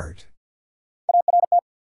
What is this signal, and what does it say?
Morse code, heard as a keyed tone, about 50 wpm, 700 Hz. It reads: ART
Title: Sentences from Top 600 Words - New Word Drill - 50wpm